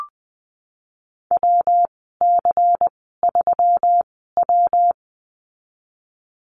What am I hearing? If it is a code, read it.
Morse code, heard as a keyed tone, about 20 wpm, spaced as for 15 wpm, 700 Hz. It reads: WC3W
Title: 122 Lesson 20g - Number 3 - Call Signs 20wpm